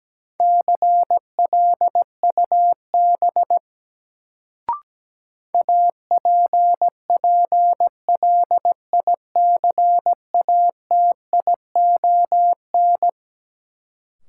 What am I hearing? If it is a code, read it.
Morse code, heard as a keyed tone, about 17 wpm, 700 Hz. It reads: CLUB APPLICATION
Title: Sentences from Top 1200 Words - New Word Drill Alt - 17wpm